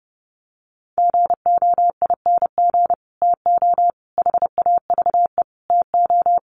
Morse code, heard as a keyed tone, about 30 wpm, 700 Hz. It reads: GOING TO HAVE TO
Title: Sets of 5 Words Encore 30wpm